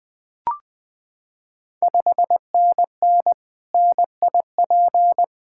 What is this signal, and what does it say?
Morse code, heard as a keyed tone, about 20 wpm, 700 Hz. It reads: 5NN NIP
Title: Ontario QSO Party Exchange 20wpm